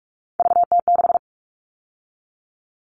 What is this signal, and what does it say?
Morse code, heard as a keyed tone, about 44 wpm, 700 Hz. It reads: VT6
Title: All Cut Numbers - 3-Digit 45wpm